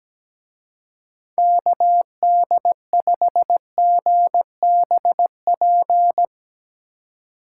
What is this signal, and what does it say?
Morse code, heard as a keyed tone, about 18 wpm, 700 Hz. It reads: KD5GBP
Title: Speed Racing - USA 2x3 Call Signs 17wpm